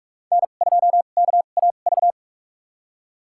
Morse code, heard as a keyed tone, about 46 wpm, spaced as for 32 wpm, 700 Hz. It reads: N2KAV
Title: USA Call Signs - 5 Characters - 2x Character Spacing - 45wpm